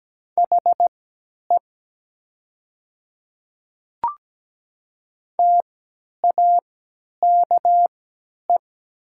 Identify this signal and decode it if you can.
Morse code, heard as a keyed tone, about 17 wpm, spaced as for 10 wpm, 700 Hz. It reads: HE TAKE
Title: 3x Character Spacing - Top 200 Words - 17wpm